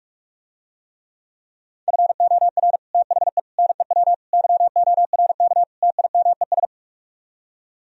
Morse code, heard as a keyed tone, about 44 wpm, 700 Hz. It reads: FOR THE NEW YORK TIMES